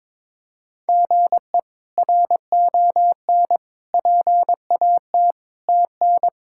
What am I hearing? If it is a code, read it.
Morse code, heard as a keyed tone, about 22 wpm, 700 Hz. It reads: GE RON PAT TN
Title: K1USN SST QSO Practice - 22wpm